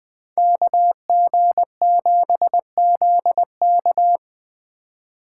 Morse code, heard as a keyed tone, about 24 wpm, 700 Hz. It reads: KG7ZK
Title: Speed Racing - USA 2x2 Call Signs 20wpm